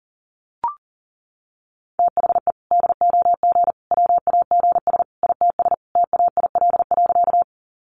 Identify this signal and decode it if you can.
Morse code, heard as a keyed tone, about 40 wpm, 700 Hz. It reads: THE DOG WAGS ITS TAIL.
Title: Easy Phrases 40wpm